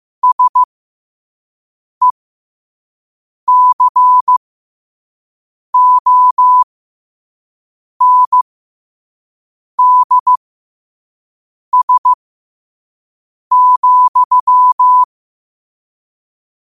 Morse code, heard as a keyed tone, about 15 wpm, spaced as for 5 wpm, 1000 Hz. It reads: SECONDS,
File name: qotdfiles / 5wpm